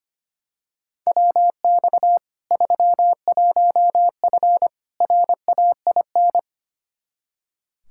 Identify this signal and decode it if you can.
Morse code, heard as a keyed tone, about 25 wpm, 700 Hz. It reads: WX 31F RAIN
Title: Practice QSO - Exchange 2 - Info - 25wpm